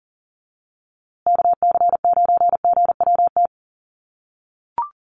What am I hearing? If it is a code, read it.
Morse code, heard as a keyed tone, about 40 wpm, 700 Hz. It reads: KC9GWT
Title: USA Call Signs Repeated 3x 40wpm